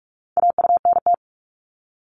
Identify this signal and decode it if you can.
Morse code, heard as a keyed tone, about 46 wpm, 700 Hz. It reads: AUNT
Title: Multi-letter Phonemes 45wpm